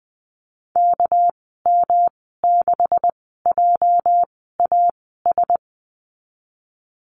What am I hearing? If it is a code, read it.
Morse code, heard as a keyed tone, about 20 wpm, spaced as for 14 wpm, 700 Hz. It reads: KM6JAS
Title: 207 Lesson 33g - Letter j - Call Signs 20wpm